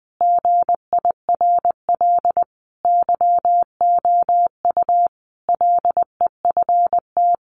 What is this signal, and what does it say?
Morse code, heard as a keyed tone, about 20 wpm, 700 Hz. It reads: GIRL YOU LEFT
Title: Sentences from Top 1500 Words 20wpm